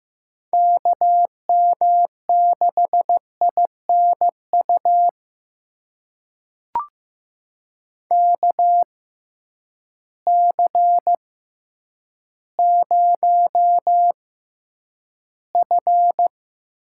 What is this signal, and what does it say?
Morse code, heard as a keyed tone, about 15 wpm, spaced as for 9 wpm, 700 Hz. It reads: KM6INU KC0F
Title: USA Call Signs - 6 Characters - 6x Character Spacing - 15wpm